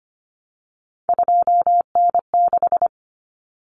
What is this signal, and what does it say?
Morse code, heard as a keyed tone, about 25 wpm, 700 Hz. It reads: 2N6
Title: Common Cut Numbers - 3-Digit 25wpm